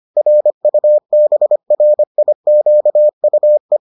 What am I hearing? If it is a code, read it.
Morse code, heard as a keyed tone, about 25 wpm, 600 Hz. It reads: RUBRIQUE
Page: bulletin UFT sous forme texte et en CW